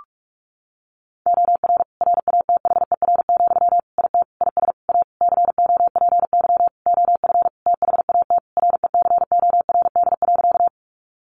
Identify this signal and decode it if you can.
Morse code, heard as a keyed tone, about 45 wpm, 700 Hz. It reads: OR RATHER, IT IS A COPY OF THAT RECORD.
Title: Sentences from Top 1400 Words 45wpm